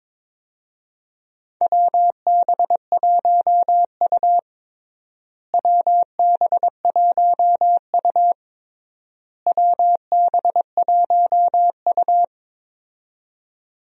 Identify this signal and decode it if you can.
Morse code, heard as a keyed tone, about 22 wpm, 700 Hz. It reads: WB1U WB1U WB1U
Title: USA Call Signs - 4 Characters - Repeated 3x - 22wpm